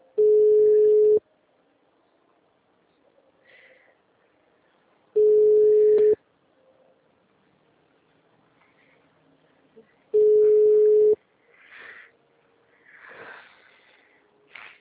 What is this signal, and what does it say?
Ringback tone (Europe, 425 Hz).
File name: out- -2010- - - .24